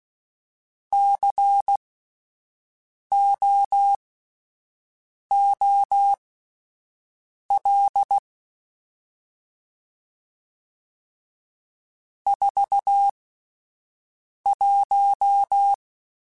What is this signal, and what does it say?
Morse code, heard as a keyed tone, about 16 wpm, spaced as for 5 wpm, 800 Hz. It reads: COOL 41
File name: Spaced Characters / Spaced Characters-mp3s / Weather 4 14 sp @10